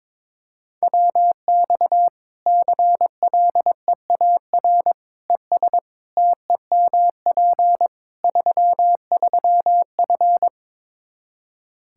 Morse code, heard as a keyed tone, about 23 wpm, 700 Hz. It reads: WX CLEAR ES TEMP 33F
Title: Practice QSO - Exchange 2 - Info - 22wpm